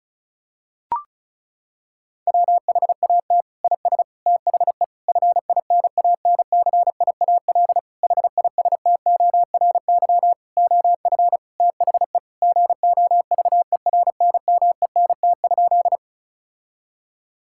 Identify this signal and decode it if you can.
Morse code, heard as a keyed tone, about 36 wpm, 700 Hz. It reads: WHAT IS THE FINANCIAL HISTORY OF THE GOVERNMENT?
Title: Sentences from Top 1000 Words - Review Part 5 - 35wpm